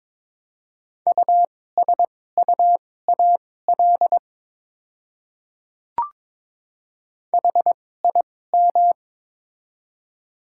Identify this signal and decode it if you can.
Morse code, heard as a keyed tone, about 22 wpm, spaced as for 16 wpm, 700 Hz. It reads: USUAL HIM
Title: 2x Character Spacing - Top 500 Words - 22wpm